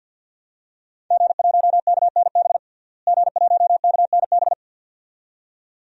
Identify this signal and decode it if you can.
Morse code, heard as a keyed tone, about 48 wpm, 700 Hz. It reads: G1KNB G1KNB
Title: DXCC-100 Call Signs - 5 Characters - Repeated 2x - 50wpm